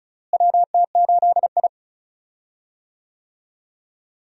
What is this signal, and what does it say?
Morse code, heard as a keyed tone, about 34 wpm, 700 Hz. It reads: WT8I
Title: USA Call Signs - 4 Characters - Repeated 2x - 35wpm